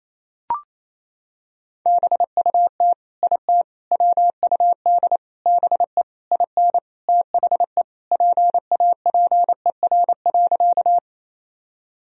Morse code, heard as a keyed tone, about 28 wpm, 700 Hz. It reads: BUT IT WUD BE IN THE PAPER.
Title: Sentences from Top 300 Words and Abbreviations 28wpm